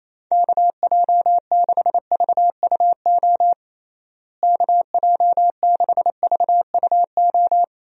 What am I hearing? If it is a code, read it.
Morse code, heard as a keyed tone, about 28 wpm, 700 Hz. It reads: KJ6VUO KJ6VUO
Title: USA Call Signs - 6 Characters - Repeated 3x - 28wpm